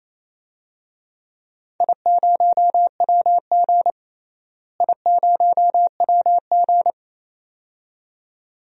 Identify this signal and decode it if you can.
Morse code, heard as a keyed tone, about 28 wpm, 700 Hz. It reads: I0WG I0WG
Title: DXCC-100 Call Signs - 4 Characters - Repeated 2x - 28wpm